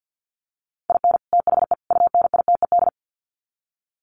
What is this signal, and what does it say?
Morse code, heard as a keyed tone, about 50 wpm, 700 Hz. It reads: IN THE UNITED
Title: Sets of 3 Words 50wpm